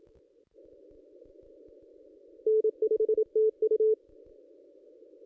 Morse code, heard as a keyed tone, about 27 wpm, 440 Hz. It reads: N5TU
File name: contests / ss-cw-k5tr-op-n5rz -2200